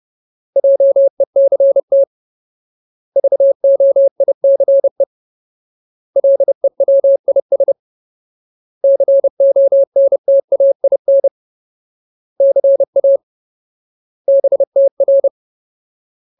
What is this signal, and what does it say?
Morse code, heard as a keyed tone, about 30 wpm, 550 Hz. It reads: JECT VOICE LEWIS CONTAIN CA BTR
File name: Ultimate Word List / 30WPM Ultimate